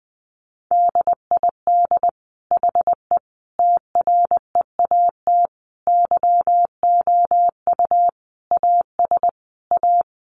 Morse code, heard as a keyed tone, about 20 wpm, 700 Hz. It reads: DID HE TREAT YOU AS A
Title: Sentences from Top 1300 Words 20wpm